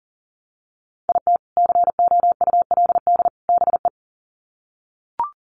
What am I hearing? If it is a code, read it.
Morse code, heard as a keyed tone, about 40 wpm, 700 Hz. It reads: IT COULD BE
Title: Sets of 3 Words 40wpm